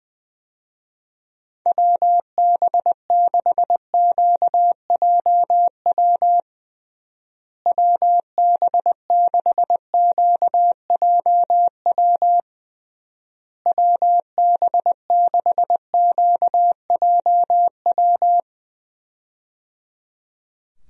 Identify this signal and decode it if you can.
Morse code, heard as a keyed tone, about 20 wpm, 700 Hz. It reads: WB6QJW WB6QJW WB6QJW
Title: USA Call Signs - 6 Characters - Repeated 3x - 20wpm